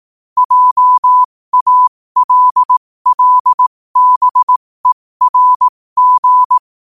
Morse code, heard as a keyed tone, about 18 wpm, spaced as for 15 wpm, 1000 Hz. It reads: JALLBERG
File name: qotdfiles / 15wpm